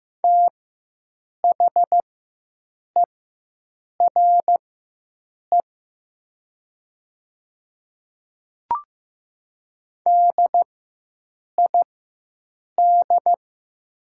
Morse code, heard as a keyed tone, about 15 wpm, spaced as for 7 wpm, 700 Hz. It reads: THERE DID
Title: 4x Character Spacing - Top 500 Words - 15wpm